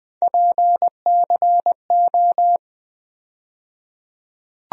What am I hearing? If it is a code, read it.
Morse code, heard as a keyed tone, about 20 wpm, 700 Hz. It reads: PCO
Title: USA Call Sign Postfixes - 3 Letters 20wpm